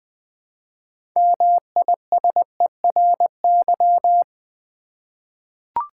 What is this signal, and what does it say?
Morse code, heard as a keyed tone, about 20 wpm, 700 Hz. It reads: MISERY
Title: 6 Letter Words - First 3 Letters Given - 20wpm